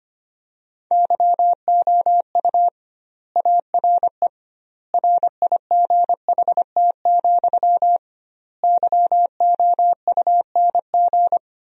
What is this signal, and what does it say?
Morse code, heard as a keyed tone, about 25 wpm, 700 Hz. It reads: YOU ARE RIGHT, YOUNG
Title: 2x Word-Spacing - Sentences from Top 400 Words - 25wpm